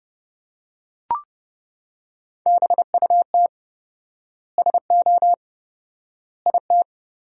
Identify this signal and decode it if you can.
Morse code, heard as a keyed tone, about 30 wpm, 700 Hz. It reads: BUT SO IT
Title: 4x Word-Spacing - Sentences from Top 100 Words - 30wpm